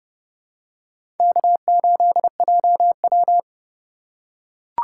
Morse code, heard as a keyed tone, about 30 wpm, 700 Hz. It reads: K8JW
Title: USA Call Signs - 4 Characters - 6x Character Spacing - 30wpm